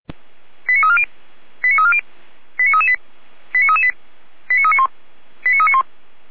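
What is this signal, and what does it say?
Five-tone selective call (ZVEI), heard as quick runs of five tones, tones of about 70 ms. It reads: 89340 89340 89338 89338 89391 89391
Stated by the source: ZVEI five-tone call received by SDR, from Feuerwehr Orlamünde fire brigade audio archive.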